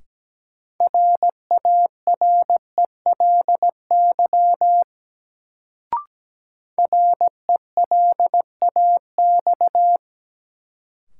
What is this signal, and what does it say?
Morse code, heard as a keyed tone, about 17 wpm, 700 Hz. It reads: RARELY RELAX
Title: Sentences from Top 1400 Words - New Word Drill Alt - 17wpm